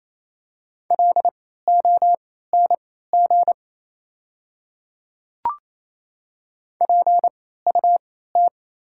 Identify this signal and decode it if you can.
Morse code, heard as a keyed tone, about 28 wpm, spaced as for 16 wpm, 700 Hz. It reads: LONG PUT
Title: 3x Character Spacing - Top 300 Words - 28wpm